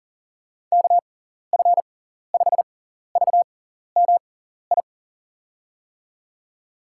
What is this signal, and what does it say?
Morse code, heard as a keyed tone, about 40 wpm, spaced as for 14 wpm, 700 Hz. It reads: KF5VMI
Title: USA Call Signs - 6 Characters - 6x Character Spacing - 40wpm